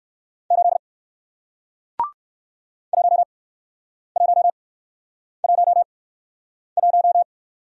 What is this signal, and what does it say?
Morse code, heard as a keyed tone, about 59 wpm, spaced as for 9 wpm, 700 Hz. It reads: 8 1111